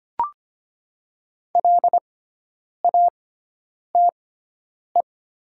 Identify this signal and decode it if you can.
Morse code, heard as a keyed tone, about 25 wpm, spaced as for 9 wpm, 700 Hz. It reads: LATE
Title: 6x Character Spacing - Top 300 Words - 25wpm